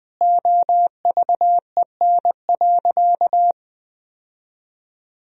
Morse code, heard as a keyed tone, about 20 wpm, 700 Hz. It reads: OVEN.
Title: Sentences from Top 1000 Words - Review Part 4 - 20wpm